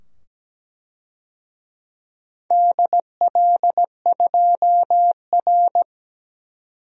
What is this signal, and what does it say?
Morse code, heard as a keyed tone, about 17 wpm, 700 Hz. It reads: DL2R
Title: DXCC-100 Call Signs - 4 Characters - 17wpm